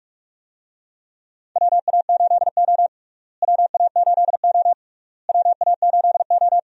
Morse code, heard as a keyed tone, about 46 wpm, 700 Hz. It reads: WA8O WA8O WA8O
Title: USA Call Signs - 4 Characters - Repeated 3x - 45wpm